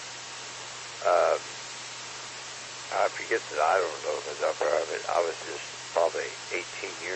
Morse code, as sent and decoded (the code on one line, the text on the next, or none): none